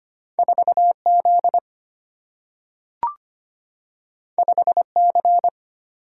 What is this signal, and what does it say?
Morse code, heard as a keyed tone, about 25 wpm, 700 Hz. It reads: 4 Z 5 C
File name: Two Letter-Number Combinations - 25wpm